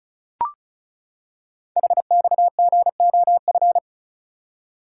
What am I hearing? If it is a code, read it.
Morse code, heard as a keyed tone, about 36 wpm, 690 Hz. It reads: HXGOF
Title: Code Groups - 5 Letters - 35wpm